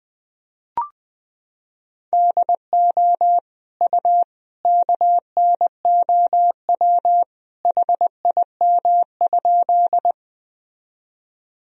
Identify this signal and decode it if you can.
Morse code, heard as a keyed tone, about 21 wpm, 700 Hz. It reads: DO U KNOW HIM?
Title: Sentences from Top 100 Words and Abbreviations 20wpm